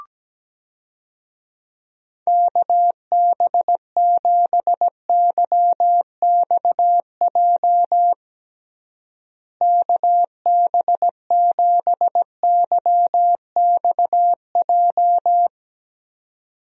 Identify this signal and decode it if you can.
Morse code, heard as a keyed tone, about 17 wpm, 700 Hz. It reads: KB7YXJ KB7YXJ
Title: USA Call Signs - 6 Characters - Repeated 2x - 17wpm